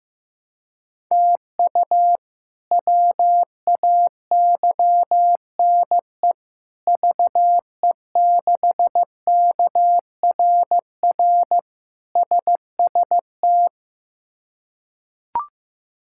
Morse code, heard as a keyed tone, about 15 wpm, 700 Hz. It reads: TU WAYNE VE6KRR SST
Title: K1USN SST QSO Practice - 15wpm